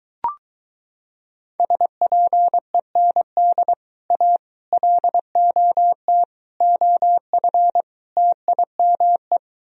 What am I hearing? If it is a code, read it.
Morse code, heard as a keyed tone, about 24 wpm, 700 Hz. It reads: SPEND A LOT OF TIME